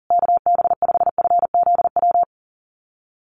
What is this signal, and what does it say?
Morse code, heard as a keyed tone, about 40 wpm, 700 Hz. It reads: KB5FZW
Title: USA Call Signs Repeated 3x 40wpm